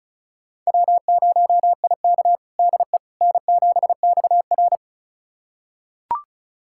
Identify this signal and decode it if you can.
Morse code, heard as a keyed tone, about 36 wpm, 700 Hz. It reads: W0IK DE N7XR
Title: USA Callsign DE Callsign - 4 Characters - 35wpm